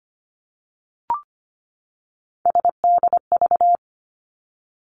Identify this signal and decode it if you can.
Morse code, heard as a keyed tone, about 25 wpm, 700 Hz. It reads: SDV
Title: Three Challenging Letter-Number Combinations 25wpm